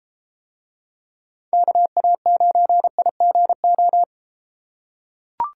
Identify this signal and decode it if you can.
Morse code, heard as a keyed tone, about 31 wpm, 700 Hz. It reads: KA9IGO